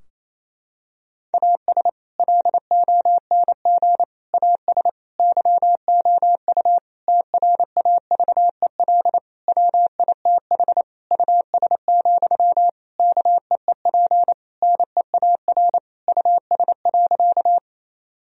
Morse code, heard as a keyed tone, about 27 wpm, 700 Hz. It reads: AS LONG AS YOU TRAVEL WITH US, KEEP NEAR US.
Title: Sentences from Top 1000 Words - Review Part 2 - 28wpm